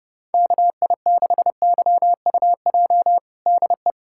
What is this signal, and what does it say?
Morse code, heard as a keyed tone, about 28 wpm, 700 Hz. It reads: KI6YUJ DE
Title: USA Callsign DE Callsign 30wpm